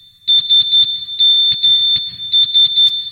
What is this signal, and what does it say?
Morse code, heard as a keyed tone, about 11 wpm, 3550 Hz. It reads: SMS